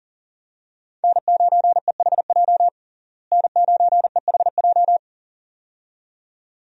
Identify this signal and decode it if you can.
Morse code, heard as a keyed tone, about 40 wpm, 700 Hz. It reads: N9EHJ N9EHJ
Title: USA Call Signs - 5 Characters - Repeated 2x - 40wpm